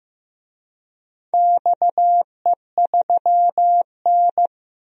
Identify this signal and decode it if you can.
Morse code, heard as a keyed tone, about 15 wpm, 700 Hz. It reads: XE3N
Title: DXCC-100 Call Signs - 4 Characters - 15wpm